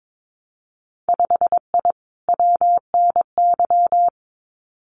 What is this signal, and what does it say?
Morse code, heard as a keyed tone, about 22 wpm, 700 Hz. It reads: 5I WNY
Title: Winter Field Day Exchanges Encore 22wpm